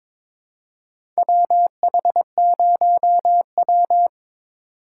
Morse code, heard as a keyed tone, about 22 wpm, 700 Hz. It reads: WH0W
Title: USA Call Signs - 4x Character Spacing - 22wpm